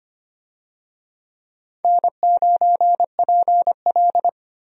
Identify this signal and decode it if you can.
Morse code, heard as a keyed tone, about 25 wpm, 700 Hz. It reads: N9PL
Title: USA Call Signs - 4 Characters - 25wpm